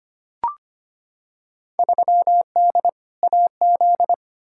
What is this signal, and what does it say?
Morse code, heard as a keyed tone, about 25 wpm, 700 Hz. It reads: 3D AZ
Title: Field Day Exchanges - 25wpm